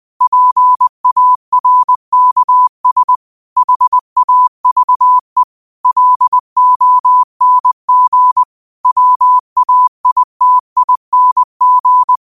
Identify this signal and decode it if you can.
Morse code, heard as a keyed tone, about 20 wpm, 1000 Hz. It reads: PARKS HAVE LONG WAITING
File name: qotdfiles / 20wpm